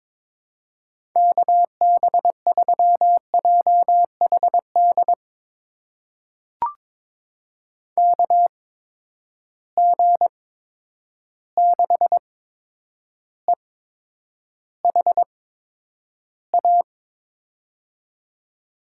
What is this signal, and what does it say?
Morse code, heard as a keyed tone, about 22 wpm, spaced as for 9 wpm, 700 Hz. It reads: KB3JHD KG6EHA
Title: USA Call Signs - 6 Characters - 8x Character Spacing - 22wpm